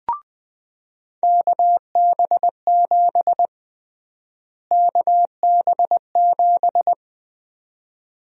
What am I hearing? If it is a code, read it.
Morse code, heard as a keyed tone, about 20 wpm, 700 Hz. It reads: KB7 KB7
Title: USA Call Sign Prefixes - Repeated 2x - 20wpm